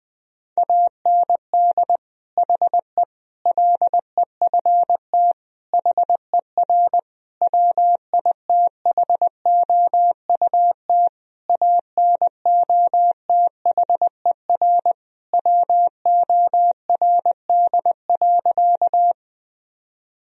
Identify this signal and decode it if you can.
Morse code, heard as a keyed tone, about 20 wpm, 700 Hz. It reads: AND HE LEFT HER WITHOUT ANOTHER WORD.
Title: Sentences from Top 1000 Words - Review Part 3 - 20wpm